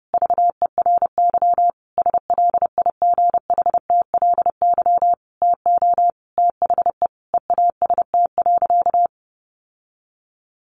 Morse code, heard as a keyed tone, about 30 wpm, 700 Hz. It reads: VERY SLIGHTLY TO THE EAST.
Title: Sentences from Top 1300 Words 30wpm